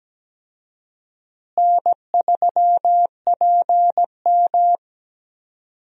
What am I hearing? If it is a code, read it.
Morse code, heard as a keyed tone, about 17 wpm, 700 Hz. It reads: N3PM
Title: USA Call Signs - 4 Characters - Repeated 2x - 17wpm